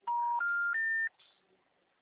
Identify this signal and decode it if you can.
Special information tone.